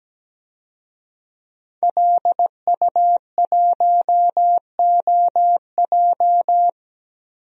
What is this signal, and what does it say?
Morse code, heard as a keyed tone, about 17 wpm, 700 Hz. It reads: LU1OJ
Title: DXCC-100 Call Signs - 5 Characters - Repeated 2x - 17wpm